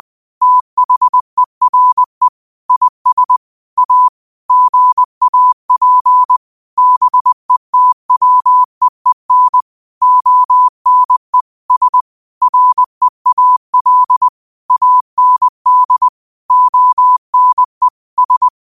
Morse code, heard as a keyed tone, about 20 wpm, 1000 Hz. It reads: THERE IS A GAP BETWEEN ONE S REAL AND ONE S
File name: qotdfiles / 20wpm